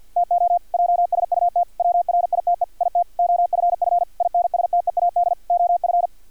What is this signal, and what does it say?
Morse code, heard as a keyed tone, about 50 wpm, 700 Hz, with radio noise. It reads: TO JSUT WRITE IT OFF INSTEAD OF
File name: archived / CRSnet